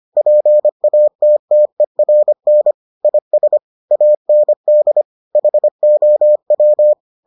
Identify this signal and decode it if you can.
Morse code, heard as a keyed tone, about 25 wpm, 600 Hz. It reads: PATTERN IS AND HOW